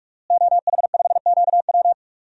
Morse code, heard as a keyed tone, about 41 wpm, 700 Hz. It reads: OH5QW
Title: DXCC-100 Call Signs - 5 Characters - Repeated 2x - 45wpm